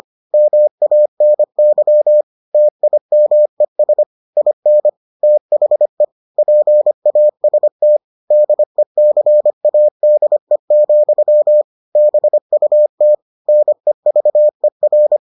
Morse code, heard as a keyed tone, about 25 wpm, 600 Hz. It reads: MANY TIMES IN THE PAST DECADE, BUT NEVER